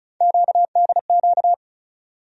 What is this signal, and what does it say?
Morse code, heard as a keyed tone, about 36 wpm, 700 Hz. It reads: QDQ
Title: USA Call Sign Postfixes - 3 Letters 35wpm